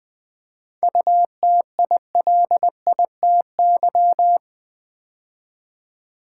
Morse code, heard as a keyed tone, about 20 wpm, 700 Hz. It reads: UTILITY
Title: 7 Letter Words - First 3 Letters Given - 20wpm